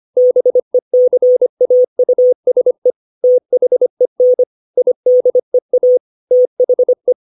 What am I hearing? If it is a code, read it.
Morse code, heard as a keyed tone, about 25 wpm, 500 Hz. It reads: BECAUSE THEN IDEA THE
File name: 632 words / 632 Words 25WPM